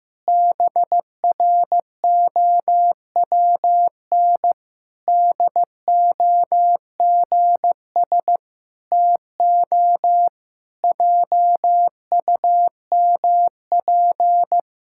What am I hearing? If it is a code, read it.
Morse code, heard as a keyed tone, about 15 wpm, 700 Hz. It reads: BROWN DOGS TO JUMP
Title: Pangram Challenge v2 - 15wpm